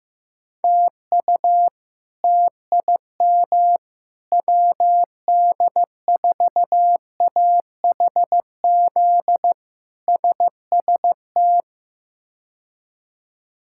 Morse code, heard as a keyed tone, about 15 wpm, 700 Hz. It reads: TU TIM WD4AHZ SST
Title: K1USN SST QSO Practice - 15wpm